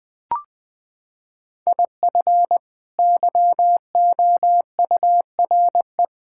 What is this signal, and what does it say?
Morse code, heard as a keyed tone, about 20 wpm, 700 Hz. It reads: IF YOURE
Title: Sets of 6 Words - 20wpm